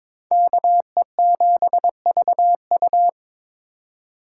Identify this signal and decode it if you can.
Morse code, heard as a keyed tone, about 26 wpm, 700 Hz. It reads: KE7VU
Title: Speed Racing - USA 2x2 Call Signs 22wpm